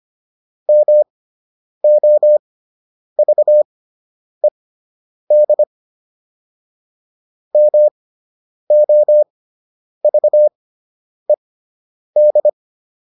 Morse code, heard as a keyed tone, about 25 wpm, spaced as for 9 wpm, 600 Hz. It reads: MOVED MOVED